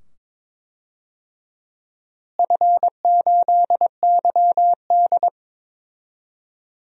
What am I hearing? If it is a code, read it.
Morse code, heard as a keyed tone, about 21 wpm, 700 Hz. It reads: F8YD
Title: DXCC-100 Call Signs - 4 Characters - Repeated 2x - 22wpm